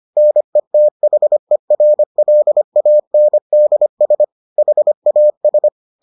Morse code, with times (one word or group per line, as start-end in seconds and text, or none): 0.2-4.2 NETHERLANDS
4.6-5.7 HAS